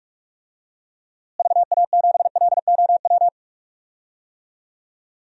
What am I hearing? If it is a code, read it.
Morse code, heard as a keyed tone, about 41 wpm, 700 Hz. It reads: VA7LOW
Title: DXCC-100 Call Signs - 6 Characters - Repeated 3x - 45wpm